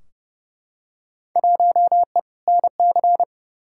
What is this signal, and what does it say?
Morse code, heard as a keyed tone, about 30 wpm, 700 Hz. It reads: 1E NC
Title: Field Day Exchanges - 30wpm